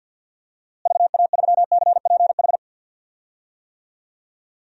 Morse code, heard as a keyed tone, about 47 wpm, 700 Hz. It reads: VA3CPH
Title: DXCC-100 Call Signs - 6 Characters - Repeated 3x - 50wpm